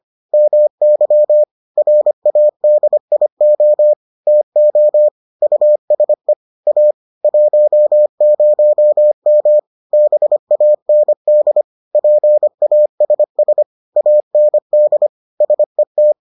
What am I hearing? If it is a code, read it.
Morse code, heard as a keyed tone, about 25 wpm, 600 Hz. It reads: MY RADIO TO USE A 10M BAND PASS AND SET